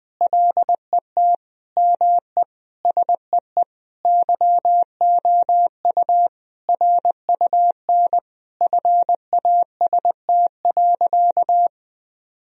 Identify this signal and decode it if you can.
Morse code, heard as a keyed tone, about 21 wpm, 700 Hz. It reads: LET ME SEE YOU RUN FAST.
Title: Sentences from Top 1000 Words - Review Part 2 - 20wpm